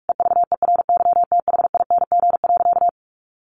Morse code, heard as a keyed tone, about 44 wpm, 700 Hz. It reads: EVERYTHING.
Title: Sentences from Top 1500 Words 45wpm